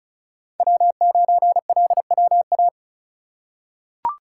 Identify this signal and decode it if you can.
Morse code, heard as a keyed tone, about 34 wpm, 700 Hz. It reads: W9LWA
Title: USA Call Signs - 2x Character Spacing - 35wpm